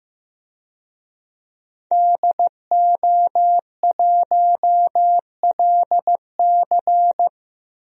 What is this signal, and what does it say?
Morse code, heard as a keyed tone, about 15 wpm, 700 Hz. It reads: DO1LC
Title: DXCC-100 Call Signs - 5 Characters - Repeated 2x - 15wpm